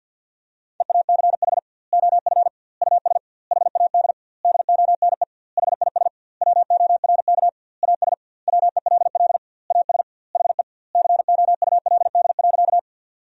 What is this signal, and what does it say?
Morse code, heard as a keyed tone, about 50 wpm, 700 Hz. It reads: EACH OF US HAD DONE HIS WORK AS WELL AS HE COULD.
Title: 2x Word-Spacing - Sentences from Top 500 Words - 50wpm